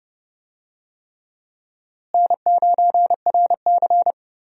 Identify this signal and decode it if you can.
Morse code, heard as a keyed tone, about 30 wpm, 700 Hz. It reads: N9RC
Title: USA Call Signs - 4 Characters - Repeated 3x - 30wpm